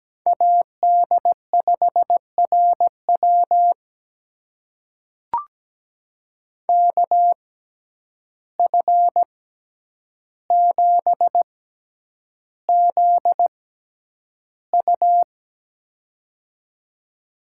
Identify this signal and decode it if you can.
Morse code, heard as a keyed tone, about 17 wpm, spaced as for 9 wpm, 700 Hz. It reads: AD5RW KF7ZU
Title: USA Call Signs - 5 Characters - 6x Character Spacing - 17wpm